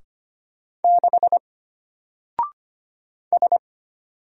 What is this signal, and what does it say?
Morse code, heard as a keyed tone, about 25 wpm, 700 Hz. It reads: 6 S
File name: Single-Letter-Number-25wpm